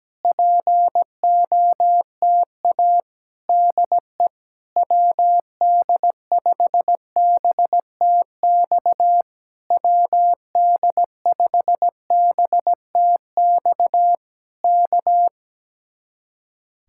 Morse code, heard as a keyed tone, about 17 wpm, 700 Hz. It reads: POTA DE WD5BTX WD5BTX K
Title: Speed-Racing Common QSO Phrases - 17wpm